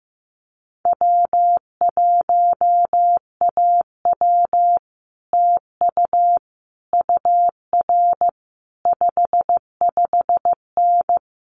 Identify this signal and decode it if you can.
Morse code, heard as a keyed tone, about 15 wpm, 700 Hz. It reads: W1AW TU UR 55N
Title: Common QSO Phrases Encore 15wpm